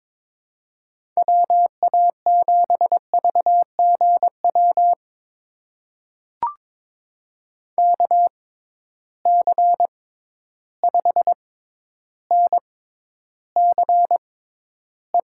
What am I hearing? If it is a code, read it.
Morse code, heard as a keyed tone, about 22 wpm, spaced as for 11 wpm, 700 Hz. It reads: WA7VGW KC5NCE
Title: USA Call Signs - 6 Characters - 6x Character Spacing - 22wpm